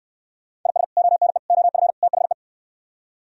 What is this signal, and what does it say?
Morse code, heard as a keyed tone, about 67 wpm, 700 Hz. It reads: IS ONE OF THE